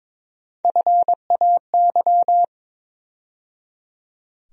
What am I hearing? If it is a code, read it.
Morse code, heard as a keyed tone, about 23 wpm, 700 Hz. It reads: FAY
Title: USA Call Sign Postfixes - 3 Letters 22wpm